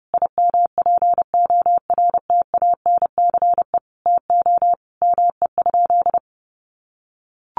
Morse code, heard as a keyed tone, about 28 wpm, 700 Hz. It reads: IMPORTANCE TO ME?
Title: Sentences from Top 1200 Words 30wpm